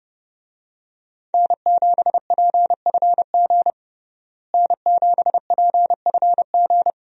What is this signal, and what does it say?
Morse code, heard as a keyed tone, about 30 wpm, 700 Hz. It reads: N7PFG N7PFG
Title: USA Call Signs - 5 Characters - Repeated 2x - 30wpm